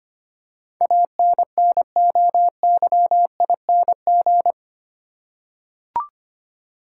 Morse code, heard as a keyed tone, about 25 wpm, 690 Hz. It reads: ANNOYING
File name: 8 Letter Words - First 3 Letters Given - 25wpm